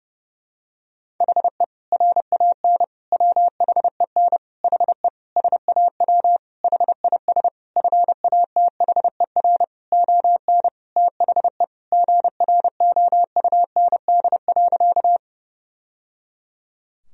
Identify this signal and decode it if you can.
Morse code, heard as a keyed tone, about 30 wpm, 700 Hz. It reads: HE RAN WHEN HE SAW HIS FATHER ON THE GROUND.
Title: Sentences from Top 1000 Words - Review Part 3 - 30wpm